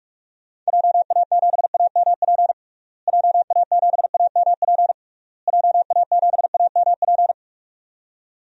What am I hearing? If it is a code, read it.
Morse code, heard as a keyed tone, about 43 wpm, 700 Hz. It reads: JA7AMP JA7AMP JA7AMP
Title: DXCC-100 Call Signs - 6 Characters - Repeated 3x - 45wpm